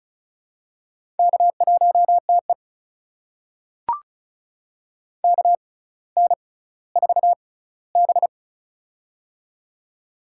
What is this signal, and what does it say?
Morse code, heard as a keyed tone, about 36 wpm, spaced as for 18 wpm, 700 Hz. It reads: K1TE KN4B
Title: USA Call Signs - 4 Characters - 6x Character Spacing - 35wpm